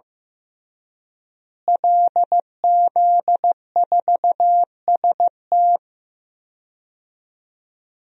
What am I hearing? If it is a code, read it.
Morse code, heard as a keyed tone, about 15 wpm, 700 Hz. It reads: LZ4ST
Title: DXCC-100 Call Signs - 5 Characters - Repeated 2x - 15wpm